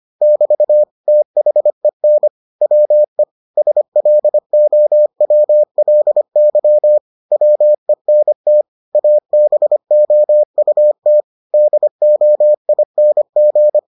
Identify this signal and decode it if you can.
Morse code, heard as a keyed tone, about 25 wpm, 600 Hz. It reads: <BT> THEN WE SLOWLY WENT ABOUT DOING